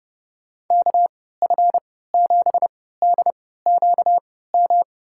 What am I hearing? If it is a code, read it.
Morse code, heard as a keyed tone, about 30 wpm, spaced as for 17 wpm, 700 Hz. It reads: KF7DQM
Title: USA Call Signs - 6 Characters - 3x Character Spacing - 30wpm